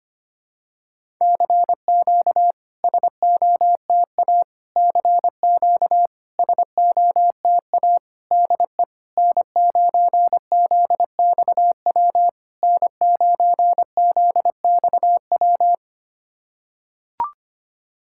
Morse code, heard as a keyed tone, about 25 wpm, 700 Hz. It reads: CQ SOTA CQ SOTA DE N9ZXW N9ZXW
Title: SOTA QSO Practice 25wpm